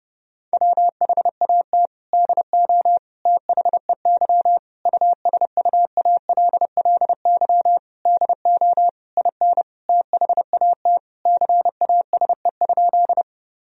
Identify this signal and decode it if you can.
Morse code, heard as a keyed tone, about 29 wpm, 700 Hz. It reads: WHAT DO THEY USUALLY DO IN THAT CASE?
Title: Sentences from Top 1000 Words - Review Part 3 - 30wpm